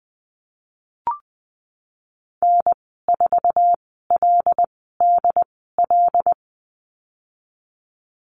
Morse code, heard as a keyed tone, about 20 wpm, spaced as for 14 wpm, 700 Hz. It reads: N4LDL
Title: 122 Lesson 20g - Number 3 - Call Signs 20wpm